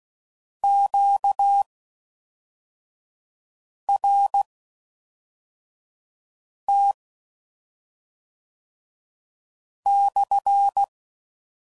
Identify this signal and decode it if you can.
Morse code, heard as a keyed tone, about 16 wpm, spaced as for 4 wpm, 800 Hz. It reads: QRT /